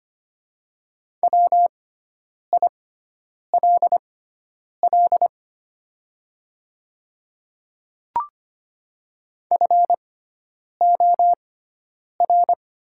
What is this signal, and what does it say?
Morse code, heard as a keyed tone, about 25 wpm, spaced as for 9 wpm, 700 Hz. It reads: WILL FOR
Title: 6x Character Spacing - Top 100 Words - 25wpm